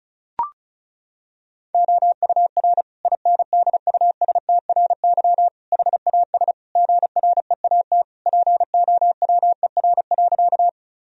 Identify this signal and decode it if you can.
Morse code, heard as a keyed tone, about 34 wpm, 700 Hz. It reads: OUR INDUSTRY HAS GREAT POWER.
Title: Sentences from Top 1000 Words - Review Part 4 - 35wpm